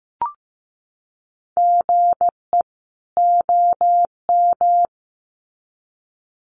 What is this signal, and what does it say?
Morse code, heard as a keyed tone, about 15 wpm, 700 Hz. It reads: GE OM
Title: Common QSO Phrases 15wpm